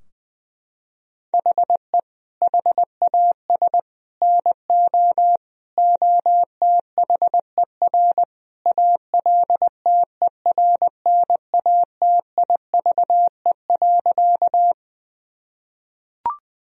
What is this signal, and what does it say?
Morse code, heard as a keyed tone, about 20 wpm, 700 Hz. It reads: HE HAS NO OTHER ALTERNATIVE.
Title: Sentences from Top 1000 Words - Review Part 4 - 20wpm